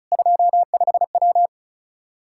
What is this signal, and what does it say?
Morse code, heard as a keyed tone, about 34 wpm, 700 Hz. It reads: 25W
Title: Practice QSO - Exchange 2 - Info - 35wpm